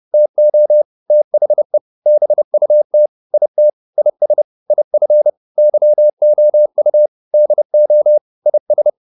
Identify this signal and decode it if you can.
Morse code, heard as a keyed tone, about 30 wpm, 600 Hz. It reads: TO THE BUT IT IS IF YOU DO IS